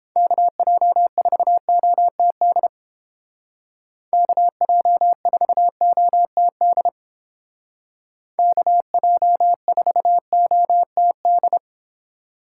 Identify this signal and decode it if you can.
Morse code, heard as a keyed tone, about 30 wpm, 690 Hz. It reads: KJ4OTD KJ4OTD KJ4OTD